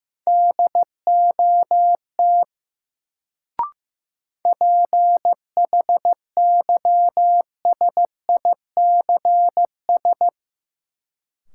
Morse code, heard as a keyed tone, about 15 wpm, 700 Hz. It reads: DOT PHYSICS
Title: Sentences from Top 1500 Words - New Word Drill Alt - 15wpm